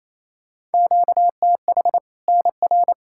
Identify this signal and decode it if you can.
Morse code, heard as a keyed tone, about 28 wpm, 700 Hz. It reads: QTH NR
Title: Practice QSO - Exchange 1 - Beginning - 28wpm